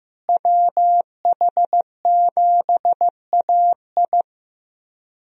Morse code, heard as a keyed tone, about 15 wpm, 700 Hz. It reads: WH7AI
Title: USA Call Signs - 5 Characters - Repeated 2x - 15wpm